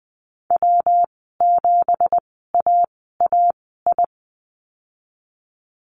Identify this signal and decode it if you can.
Morse code, heard as a keyed tone, about 20 wpm, spaced as for 15 wpm, 700 Hz. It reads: W7AAI
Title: 173 Lesson 28g - Number 9 - Call Signs 20wpm